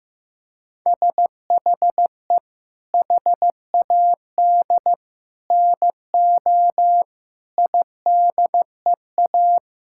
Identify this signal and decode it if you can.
Morse code, heard as a keyed tone, about 16 wpm, 700 Hz. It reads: SHE HAD NO IDEA
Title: Sentences from Top 1000 Words - Review Part 6 - 15wpm